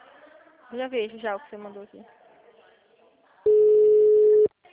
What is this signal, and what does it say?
Ringback tone (Europe, 425 Hz), with speech.